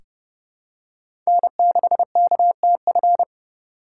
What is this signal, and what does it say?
Morse code, heard as a keyed tone, about 30 wpm, 700 Hz. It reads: N6KTF
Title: USA Call Signs - 5 Characters - 4x Character Spacing - 30wpm